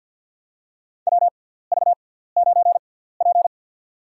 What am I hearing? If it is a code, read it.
Morse code, heard as a keyed tone, about 52 wpm, spaced as for 17 wpm, 700 Hz. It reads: WV9P
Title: USA Call Signs - 4 Characters - 6x Character Spacing - 50wpm